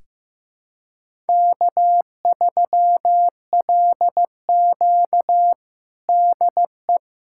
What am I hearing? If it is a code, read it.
Morse code, heard as a keyed tone, about 15 wpm, 700 Hz. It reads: K3LQ DE
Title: USA Callsign DE Callsign - 4 Characters - 15wpm